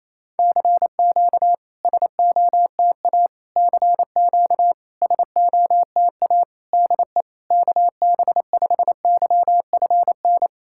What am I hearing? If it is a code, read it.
Morse code, heard as a keyed tone, about 28 wpm, 690 Hz. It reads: CQ SOTA CQ SOTA DE KB5YFN
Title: SOTA QSO Practice 28wpm